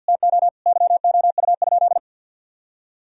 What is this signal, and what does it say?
Morse code, heard as a keyed tone, about 51 wpm, 700 Hz. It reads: TO YOU?
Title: Sentences from Top 1000 Words - Review Part 3 - 50wpm